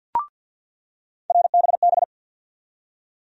Morse code, heard as a keyed tone, about 53 wpm, 700 Hz. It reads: RBB
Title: USA Call Sign Postfixes - 3 Letters 50wpm